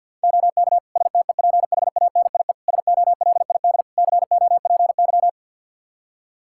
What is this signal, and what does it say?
Morse code, heard as a keyed tone, about 52 wpm, 700 Hz. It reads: OK STEPHANIE SOLID COPY
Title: Practice QSO - Exchange 3 - More Info - 50wpm